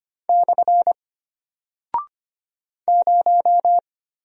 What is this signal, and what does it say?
Morse code, heard as a keyed tone, about 25 wpm, 700 Hz. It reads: / 0
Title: Single Character (Letter:Number:Punctuation:Prosign) Alt 25wpm